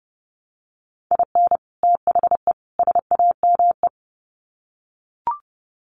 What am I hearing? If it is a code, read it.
Morse code, heard as a keyed tone, about 30 wpm, 700 Hz. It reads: IN THE SAME
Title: Sets of 3 Words 30wpm